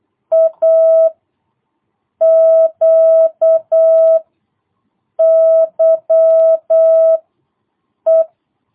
Morse code, heard as a keyed tone, about 8 wpm, 660 Hz. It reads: A Q Y E
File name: jh8cbh / 2022cw